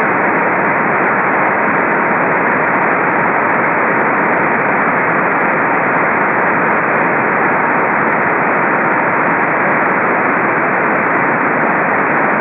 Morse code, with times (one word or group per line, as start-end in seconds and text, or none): none